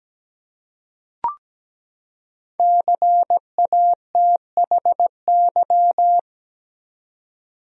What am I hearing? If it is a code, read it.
Morse code, heard as a keyed tone, about 17 wpm, 700 Hz. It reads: CATHY
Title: USA Common Names - 17wpm